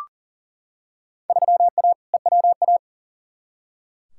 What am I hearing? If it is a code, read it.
Morse code, heard as a keyed tone, about 40 wpm, 700 Hz. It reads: 3A EWA
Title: Field Day Exchanges Encore - 40wpm